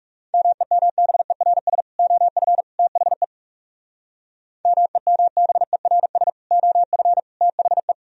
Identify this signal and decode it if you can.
Morse code, heard as a keyed tone, about 37 wpm, 700 Hz. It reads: MEMBERS OF THE MEMBERS OF THE
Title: Speed-Racing - Sets of 3 Words 30wpm